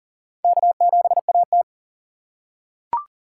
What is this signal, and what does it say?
Morse code, heard as a keyed tone, about 40 wpm, 700 Hz. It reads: K7AT
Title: USA Call Signs - 4 Characters - 3x Character Spacing - 40wpm